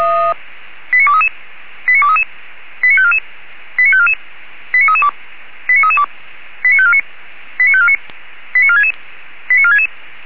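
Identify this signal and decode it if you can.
Five-tone selective call (ZVEI), heard as quick runs of five tones, tones of about 70 ms. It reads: 89233 89233 89544 89544 89332 89332 89549 89549 89466 89466
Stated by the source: Five-tone ZVEI selective call received by SDR.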